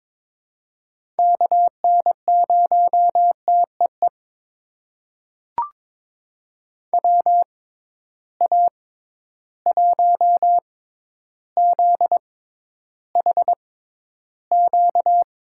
Morse code, heard as a keyed tone, about 22 wpm, spaced as for 11 wpm, 700 Hz. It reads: KN0TEE WA1ZHQ
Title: USA Call Signs - 6 Characters - 6x Character Spacing - 22wpm